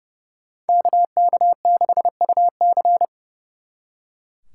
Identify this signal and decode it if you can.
Morse code, heard as a keyed tone, about 40 wpm, 700 Hz. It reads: KK6UC